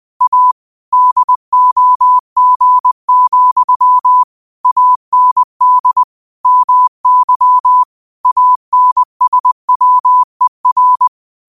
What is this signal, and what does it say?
Morse code, heard as a keyed tone, about 20 wpm, 1000 Hz. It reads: A DOG, AND MY ANSWER
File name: qotdfiles / 20wpm